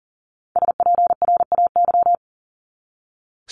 Morse code, heard as a keyed tone, about 40 wpm, 700 Hz. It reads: SPRAY
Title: Top 1751-2000 Words 40wpm